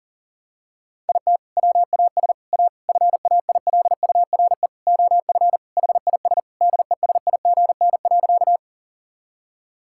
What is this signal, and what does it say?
Morse code, heard as a keyed tone, about 40 wpm, 700 Hz. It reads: IT WAS A FAILURE OF HIS DESIGN.
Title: Sentences from Top 1000 Words - Review Part 6 - 40wpm